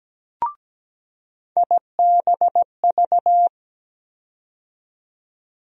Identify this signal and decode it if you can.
Morse code, heard as a keyed tone, about 17 wpm, 700 Hz. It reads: IBV
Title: USA Call Sign Postfixes - 3 Letters 17wpm